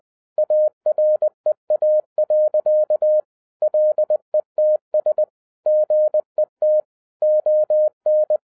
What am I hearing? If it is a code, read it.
Morse code, heard as a keyed tone, about 20 wpm, 610 Hz. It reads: AREA. LETS GET ON